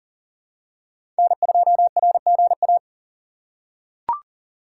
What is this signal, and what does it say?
Morse code, heard as a keyed tone, about 40 wpm, 700 Hz. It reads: N2RGA
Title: USA Call Signs - 5 Characters - 3x Character Spacing - 40wpm